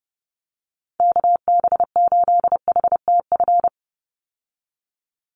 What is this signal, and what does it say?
Morse code, heard as a keyed tone, about 30 wpm, 700 Hz. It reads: KB8HTF
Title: USA Call Signs 30wpm